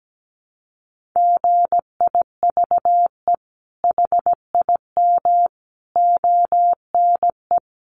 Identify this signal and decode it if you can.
Morse code, heard as a keyed tone, about 17 wpm, 700 Hz. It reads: GIVE HIM ONE
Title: Sentences from Top 1200 Words 17wpm